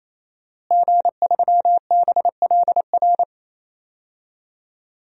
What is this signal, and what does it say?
Morse code, heard as a keyed tone, about 27 wpm, 700 Hz. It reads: G3BLR
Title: DXCC-100 Call Signs - 5 Characters - Repeated 3x - 28wpm